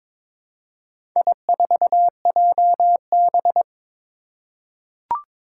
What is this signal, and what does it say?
Morse code, heard as a keyed tone, about 22 wpm, 700 Hz. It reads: I4JB
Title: Four Challenging Letter-Number Combinations 22wpm